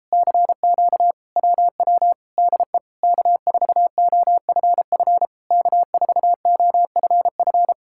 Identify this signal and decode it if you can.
Morse code, heard as a keyed tone, about 37 wpm, 700 Hz. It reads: CQ WW DE K4OFF K4OFF